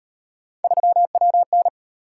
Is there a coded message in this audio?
Yes: Morse code from 0.6 to 1.7 s, heard as a keyed tone, about 36 wpm, 700 Hz. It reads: 3WN